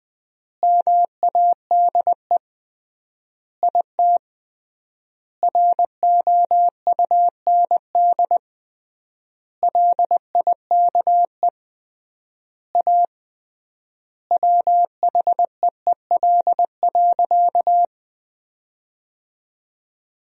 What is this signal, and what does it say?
Morse code, heard as a keyed tone, about 20 wpm, 700 Hz. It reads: MADE IT ROUND LIKE A WHEEL.
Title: 3x Word-Spacing - Sentences from Top 500 Words - 20wpm